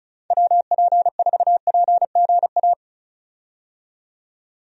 Morse code, heard as a keyed tone, about 34 wpm, 700 Hz. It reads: WP4PGA
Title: USA Call Signs - 6 Characters - 35wpm